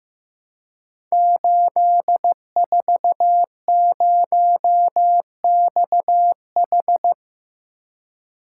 Morse code, heard as a keyed tone, about 15 wpm, 700 Hz. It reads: 840XH
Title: Code Groups - 5 Letter:Numbers - 15wpm